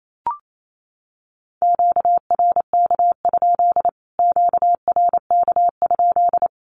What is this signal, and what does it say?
Morse code, heard as a keyed tone, about 28 wpm, 700 Hz. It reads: QRK? QRK?
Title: Q-Codes Repeated 3x 28wpm